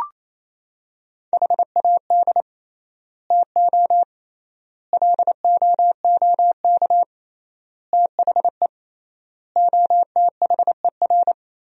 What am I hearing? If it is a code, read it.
Morse code, heard as a keyed tone, about 28 wpm, 700 Hz. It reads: HAD TO LOOK THE OTHER
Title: 3x Word-Spacing - Sentences from Top 100 Words - 28wpm